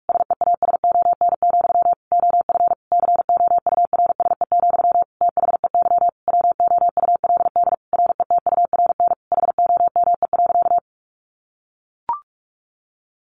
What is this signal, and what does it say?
Morse code, heard as a keyed tone, about 44 wpm, 700 Hz. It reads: SEASON, OF COURSE, THEY WOULD RETURN HOME.
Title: Sentences from Top 1400 Words 45wpm